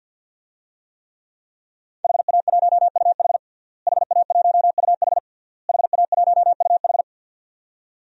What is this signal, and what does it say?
Morse code, heard as a keyed tone, about 49 wpm, 700 Hz. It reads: HA1UH HA1UH HA1UH
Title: DXCC-100 Call Signs - 5 Characters - Repeated 3x - 50wpm